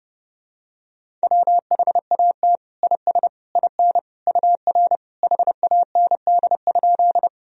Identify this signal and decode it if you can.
Morse code, heard as a keyed tone, about 30 wpm, 700 Hz. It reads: WHAT IS IN UR HAND?
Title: Sentences from Top 200 Words and Abbreviations 30wpm